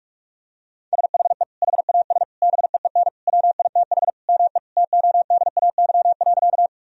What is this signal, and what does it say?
Morse code, heard as a keyed tone, about 47 wpm, 700 Hz. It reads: SHE HAS BEEN WITH ME TODAY.
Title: Sentences from Top 1000 Words - Review Part 4 - 45wpm